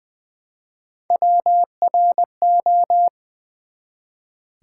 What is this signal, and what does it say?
Morse code, heard as a keyed tone, about 20 wpm, 700 Hz. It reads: WRO
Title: Build A Word - 1 Letter At A Time - 5-Letter Words - 20wpm